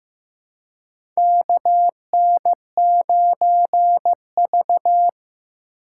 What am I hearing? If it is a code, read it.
Morse code, heard as a keyed tone, about 15 wpm, 700 Hz. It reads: KN9V
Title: USA Call Signs - 4 Characters - 4x Character Spacing - 15wpm